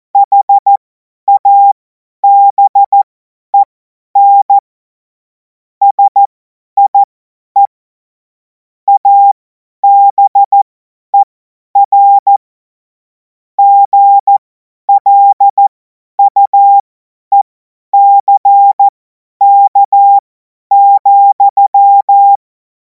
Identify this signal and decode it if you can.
Morse code, heard as a keyed tone, about 14 wpm, spaced as for 10 wpm, 800 Hz. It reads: HABEN SIE ABER GLUECK,
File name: Morsekurs / Witze / Witz 10. Februar 2026 50 F70